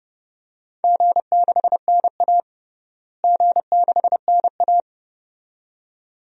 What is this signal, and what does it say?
Morse code, heard as a keyed tone, about 30 wpm, 700 Hz. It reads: G6NA G6NA
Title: DXCC-100 Call Signs - 4 Characters - Repeated 2x - 30wpm